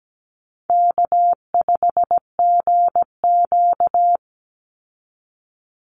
Morse code, heard as a keyed tone, about 17 wpm, 700 Hz. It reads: K5GQ
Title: Easy Call Signs - 17wpm